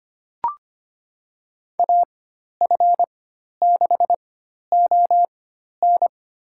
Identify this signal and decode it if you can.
Morse code, heard as a keyed tone, about 25 wpm, spaced as for 12 wpm, 700 Hz. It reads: AF6ON
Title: USA Call Signs - 5 Characters - 4x Character Spacing - 25wpm